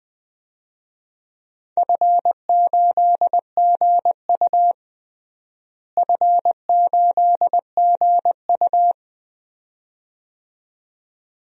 Morse code, heard as a keyed tone, about 20 wpm, 700 Hz. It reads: F8GU F8GU
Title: DXCC-100 Call Signs - 4 Characters - Repeated 2x - 20wpm